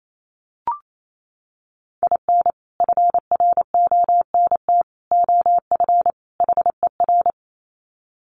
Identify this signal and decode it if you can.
Morse code, heard as a keyed tone, about 28 wpm, 700 Hz. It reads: IN FRONT OF HER
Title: Sets of 4 Words 28wpm